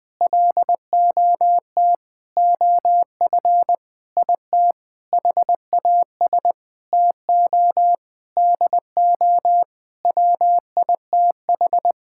Morse code, heard as a keyed tone, about 20 wpm, 700 Hz. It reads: LOT OF IT HAS TO DO WITH
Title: Sentences with 7-Word Phrases - 20wpm